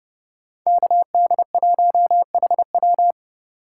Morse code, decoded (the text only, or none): KD1HW